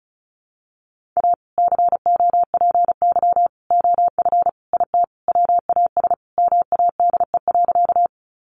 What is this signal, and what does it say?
Morse code, heard as a keyed tone, about 34 wpm, 700 Hz. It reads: A COPY OF IT WAS MADE.
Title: Sentences from Top 1400 Words 35wpm